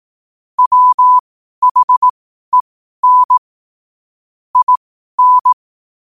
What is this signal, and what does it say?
Morse code, heard as a keyed tone, about 18 wpm, spaced as for 12 wpm, 1000 Hz. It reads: WHEN IN
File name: qotdfiles / 12wpm